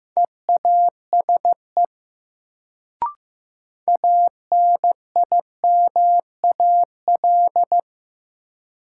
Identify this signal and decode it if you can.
Morse code, heard as a keyed tone, about 15 wpm, 700 Hz. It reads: EASE ANIMAL
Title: Sentences from Top 1000 Words - Review Part 1 - New Word Drill - 15wpm